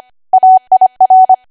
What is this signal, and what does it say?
Morse code, heard as a keyed tone, about 25 wpm, 750 Hz. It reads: AIR
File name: Archive / 25 WPM / 25WPM